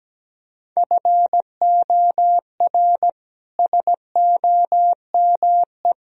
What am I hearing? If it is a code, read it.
Morse code, heard as a keyed tone, about 17 wpm, 700 Hz. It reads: FOR SOME
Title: Sentences from Top 1000 Words - Review Part 2 - 17wpm